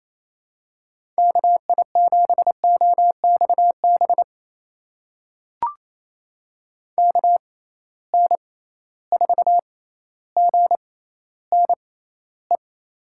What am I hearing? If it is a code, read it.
Morse code, heard as a keyed tone, about 28 wpm, spaced as for 14 wpm, 700 Hz. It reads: KI7OXB KN4GNE
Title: USA Call Signs - 6 Characters - 6x Character Spacing - 28wpm